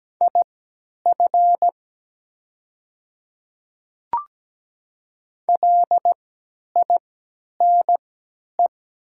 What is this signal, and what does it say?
Morse code, heard as a keyed tone, about 17 wpm, spaced as for 10 wpm, 700 Hz. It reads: IF LINE
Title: 3x Character Spacing - Top 300 Words - 17wpm